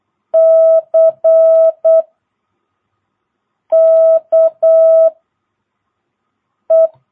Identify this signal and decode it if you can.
Morse code, heard as a keyed tone, about 8 wpm, 660 Hz. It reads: C K E
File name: jh8cbh / 2022cw